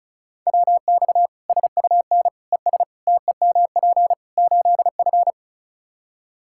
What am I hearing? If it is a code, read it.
Morse code, heard as a keyed tone, about 36 wpm, 700 Hz. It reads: WX SUN ES TEMP 8F
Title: Practice QSO - Exchange 2 - Info - 35wpm